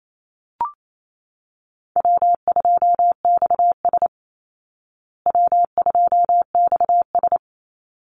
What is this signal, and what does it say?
Morse code, heard as a keyed tone, about 28 wpm, 700 Hz. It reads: W2XS W2XS
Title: Call Signs Repeated 3x - 28wpm